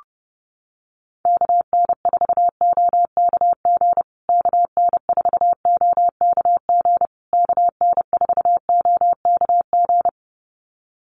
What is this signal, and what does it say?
Morse code, heard as a keyed tone, about 30 wpm, 700 Hz. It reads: KN4OKG KN4OKG KN4OKG
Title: USA Call Signs Repeated 3x 30wpm